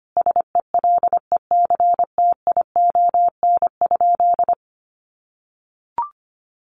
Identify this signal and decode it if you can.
Morse code, heard as a keyed tone, about 25 wpm, 700 Hz. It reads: SELECTION?
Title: Sentences from Top 1400 Words 25wpm